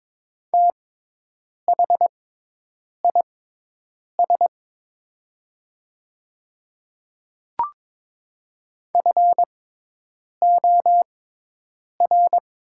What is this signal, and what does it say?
Morse code, heard as a keyed tone, about 22 wpm, spaced as for 8 wpm, 700 Hz. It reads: THIS FOR
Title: 6x Character Spacing - Top 100 Words - 22wpm